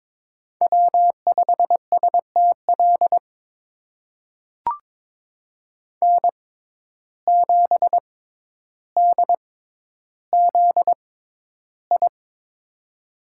Morse code, heard as a keyed tone, about 22 wpm, spaced as for 11 wpm, 700 Hz. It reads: W5STL N7DZI
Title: USA Call Signs - 5 Characters - 6x Character Spacing - 22wpm